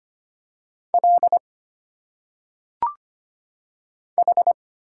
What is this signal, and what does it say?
Morse code, heard as a keyed tone, about 25 wpm, 700 Hz. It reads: L H
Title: Single Letters - Phonetic Pronunciation 25wpm